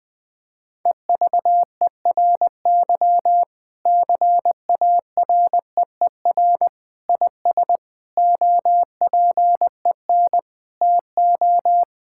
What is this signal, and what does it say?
Morse code, heard as a keyed tone, about 21 wpm, 700 Hz. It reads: EVERY CAREER IS OPEN TO
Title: Sentences from Top 1000 Words - Review Part 6 - 20wpm